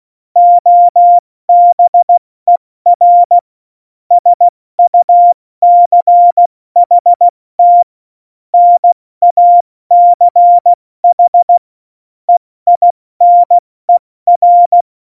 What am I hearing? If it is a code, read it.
Morse code, heard as a keyed tone, about 16 wpm, spaced as for 14 wpm, 700 Hz. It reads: OBER SUCHT NACH EINER
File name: Morsekurs / Witze / Witz 09. April 2019, 70 F80